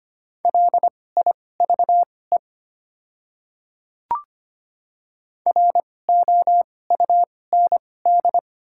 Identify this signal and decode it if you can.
Morse code, heard as a keyed tone, about 25 wpm, spaced as for 18 wpm, 700 Hz. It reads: LIVE ROUND
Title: 2x Character Spacing - Top 300 Words - 25wpm